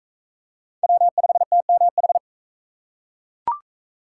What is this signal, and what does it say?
Morse code, heard as a keyed tone, about 36 wpm, 700 Hz. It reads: W5TMH